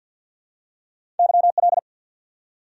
Morse code, heard as a keyed tone, about 54 wpm, 700 Hz. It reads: YL
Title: Top 90 QSO Elements 50wpm